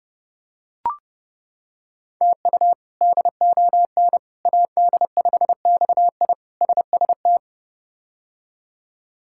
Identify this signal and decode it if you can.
Morse code, heard as a keyed tone, about 32 wpm, 700 Hz. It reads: TU DON AD5XI SST
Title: K1USN SST QSO Practice - 30wpm